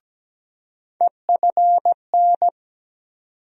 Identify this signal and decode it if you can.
Morse code, heard as a keyed tone, about 17 wpm, 700 Hz. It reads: EFN
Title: USA Call Sign Postfixes - 3 Letters 17wpm